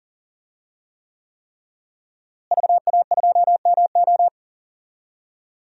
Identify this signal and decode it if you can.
Morse code, heard as a keyed tone, about 40 wpm, 700 Hz. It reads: VA2MO
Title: DXCC-100 Call Signs - 5 Characters - Repeated 3x - 40wpm